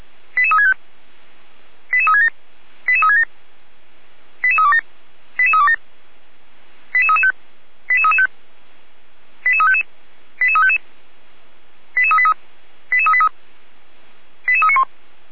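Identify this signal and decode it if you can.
Five-tone selective call (ZVEI), heard as quick runs of five tones, tones of about 70 ms. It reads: 88365 88367 88367 88327 88327 88335 88335 88355 88355 88383 88383 88391
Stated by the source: ZVEI five-tone call received on SDR.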